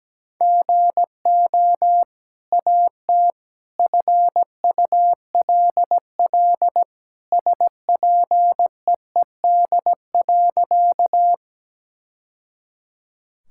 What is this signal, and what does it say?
Morse code, heard as a keyed tone, about 17 wpm, 700 Hz. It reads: GO AT FULL SPEED.
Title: Sentences from Top 1000 Words - Review Part 5 - 17wpm